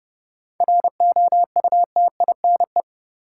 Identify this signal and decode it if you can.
Morse code, heard as a keyed tone, about 30 wpm, 700 Hz. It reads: ROUTINE
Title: Seven Letter Words - 30wpm